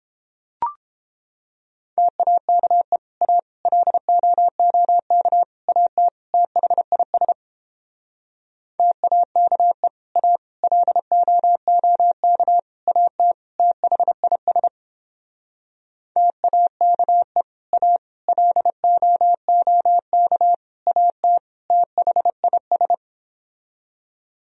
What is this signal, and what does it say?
Morse code, heard as a keyed tone, about 29 wpm, 700 Hz. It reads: TAKE A LOOK AT THIS TAKE A LOOK AT THIS TAKE A LOOK AT THIS